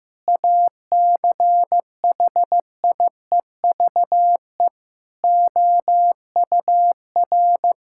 Morse code, heard as a keyed tone, about 16 wpm, 700 Hz. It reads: ACHIEVE OUR
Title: Sentences from Top 1000 Words - Review Part 6 - 15wpm